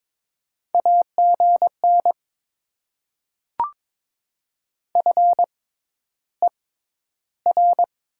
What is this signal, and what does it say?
Morse code, heard as a keyed tone, about 22 wpm, spaced as for 11 wpm, 700 Hz. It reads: AGN FER
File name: Top 20 QSO Elements - 6x Character Spacing - 22wpm